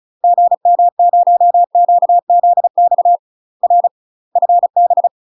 Morse code, heard as a keyed tone, about 34 wpm, 700 Hz. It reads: GM0QZX R FB
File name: UK MASTER QSOs TRAINING / 35WPM / UK MASTER QSOs 0034